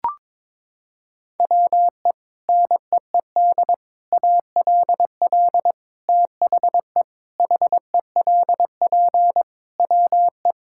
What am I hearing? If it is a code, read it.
Morse code, heard as a keyed tone, about 22 wpm, 700 Hz. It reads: WE NEED ALL THE HELP WE
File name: practice / 22wpm / Sets of 8 Words - 22wpm